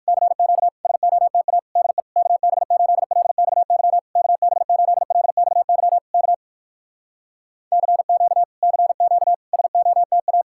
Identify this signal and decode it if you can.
Morse code, heard as a keyed tone, about 59 wpm, 700 Hz. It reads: CQ SOTA DE KB8LXY KB8LXY K CQ CQ SOTA